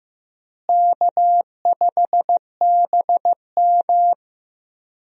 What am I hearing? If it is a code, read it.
Morse code, heard as a keyed tone, about 16 wpm, 700 Hz. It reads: K5BM
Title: USA Call Signs - 4 Characters - Repeated 3x - 15wpm